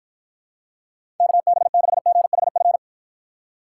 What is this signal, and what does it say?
Morse code, heard as a keyed tone, about 53 wpm, 700 Hz. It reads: KB6GHF